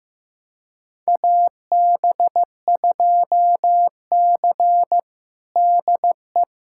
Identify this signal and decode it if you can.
Morse code, heard as a keyed tone, about 16 wpm, 700 Hz. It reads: AB2C DE
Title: USA Callsign DE Callsign - 4 Characters - 15wpm